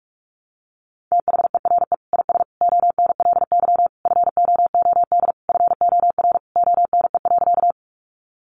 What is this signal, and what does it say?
Morse code, heard as a keyed tone, about 44 wpm, 700 Hz. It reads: THERE IS ONLY FOOD FOR ONE.
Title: Sentences from Top 300 Words 45wpm